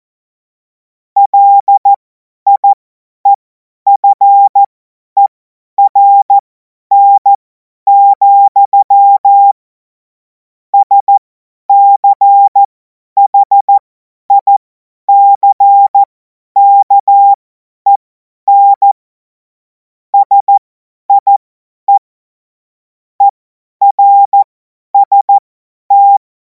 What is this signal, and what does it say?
Morse code, heard as a keyed tone, about 14 wpm, spaced as for 10 wpm, 800 Hz. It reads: LIEFERN, SCHICKEN SIE ERST
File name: Morsekurs / Witze / Witz 28. April 2026 50 F70